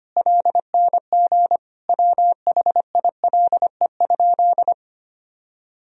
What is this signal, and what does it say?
Morse code, heard as a keyed tone, about 25 wpm, 700 Hz. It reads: LNG WHILE?
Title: Sentences from Top 300 Words and Abbreviations 25wpm